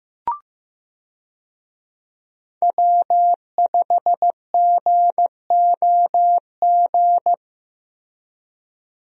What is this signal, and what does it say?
Morse code, heard as a keyed tone, about 15 wpm, 700 Hz. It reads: W5GOG
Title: USA Call Signs - 5 Characters - Repeated 2x - 15wpm